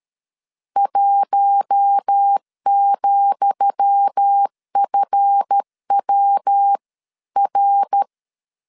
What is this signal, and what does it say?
Morse code, heard as a keyed tone, about 13 wpm, 800 Hz. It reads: 1,FWR
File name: LEZIONI / 10 SIGNS / 10D COMA 12wpm 2544